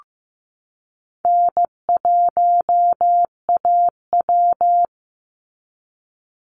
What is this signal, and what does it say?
Morse code, heard as a keyed tone, about 15 wpm, 700 Hz. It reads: N1AW
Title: NA SOTA Chaser Callsigns 15wpm